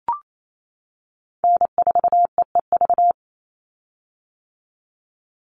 Morse code, heard as a keyed tone, about 28 wpm, 700 Hz. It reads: N4EEV
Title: Call Signs - 28wpm